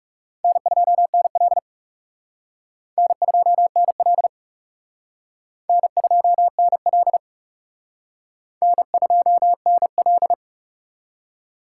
Morse code, heard as a keyed tone, about 36 wpm, 700 Hz. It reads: N2NL N2NL N2NL N2NL
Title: Speed Racing - USA 1x2 Call Signs 30wpm